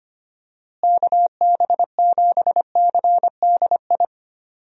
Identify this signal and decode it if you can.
Morse code, heard as a keyed tone, about 25 wpm, 700 Hz. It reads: KB7CDI
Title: USA Call Signs - 6 Characters - Repeated 3x - 25wpm